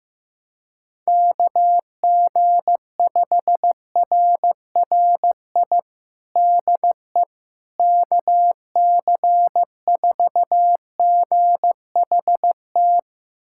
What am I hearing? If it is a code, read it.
Morse code, heard as a keyed tone, about 15 wpm, 700 Hz. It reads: KG5RRI DE KC4GHT
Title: USA Callsign DE Callsign - 6 Characters - 15wpm